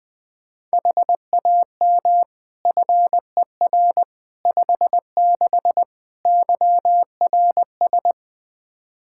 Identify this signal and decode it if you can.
Morse code, heard as a keyed tone, about 20 wpm, 700 Hz. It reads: HAM FER 56 YRS
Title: Practice QSO - Exchange 3 - More Info - 20wpm